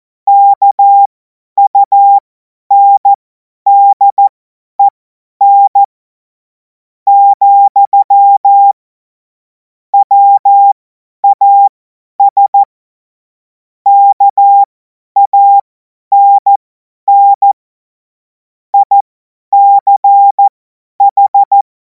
Morse code, heard as a keyed tone, about 14 wpm, spaced as for 9 wpm, 800 Hz. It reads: KUNDEN, WAS KANN ICH
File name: Morsekurs / Witze / Witz 17. Februar 2026 50 F70